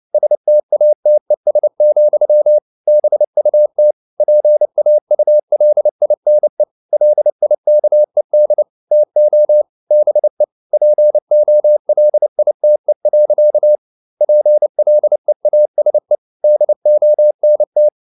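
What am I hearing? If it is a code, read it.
Morse code, heard as a keyed tone, about 29 wpm, 600 Hz. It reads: STATES, BUT PAULINE LIKED TO BE POLITE. PLEASE DONT
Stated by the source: Morse code practice at 28 wpm, text of Ken Follett's novel Never in English, from CW med Gnister.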